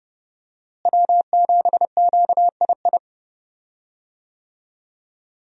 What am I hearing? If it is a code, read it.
Morse code, heard as a keyed tone, about 30 wpm, 700 Hz. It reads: W7QII
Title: USA Call Signs - 5 Characters - Repeated 2x - 30wpm